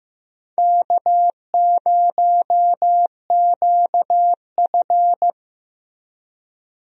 Morse code, heard as a keyed tone, about 15 wpm, 700 Hz. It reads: K0QF
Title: USA Call Signs - 4 Characters - Repeated 2x - 15wpm